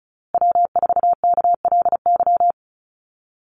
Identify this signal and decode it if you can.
Morse code, heard as a keyed tone, about 34 wpm, 700 Hz. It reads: W4KLY
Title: Call Signs Repeated 2x - 35wpm